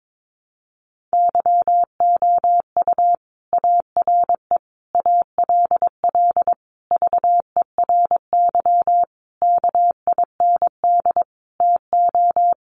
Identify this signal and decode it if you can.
Morse code, heard as a keyed tone, about 22 wpm, 700 Hz. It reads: YOU ARE ALL VERY KIND TO
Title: Sentences from Top 200 Words 22wpm